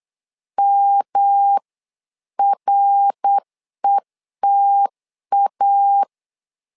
Morse code, heard as a keyed tone, about 8 wpm, 800 Hz. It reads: MRETA
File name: LEZIONI / 3 MWRO / 3 MWR 8wpm 2444